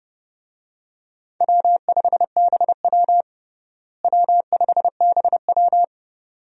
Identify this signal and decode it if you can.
Morse code, heard as a keyed tone, about 32 wpm, 700 Hz. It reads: W5BW W5BW
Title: USA Call Signs - 4 Characters - Repeated 2x - 30wpm